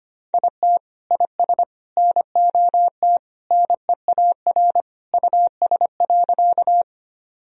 Morse code, heard as a keyed tone, about 25 wpm, 700 Hz. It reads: IT IS NOT NEAR US.
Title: Sentences from Top 1000 Words - Review Part 1 - 25wpm